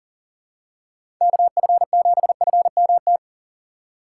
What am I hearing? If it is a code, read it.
Morse code, heard as a keyed tone, about 40 wpm, 700 Hz. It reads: KF7FMT
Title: USA Call Signs - 6 Characters - 2x Character Spacing - 40wpm